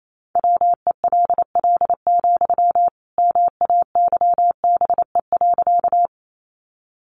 Morse code, heard as a keyed tone, about 28 wpm, 700 Hz. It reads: WELL, MAYBE.
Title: Sentences from Top 1000 Words 28wpm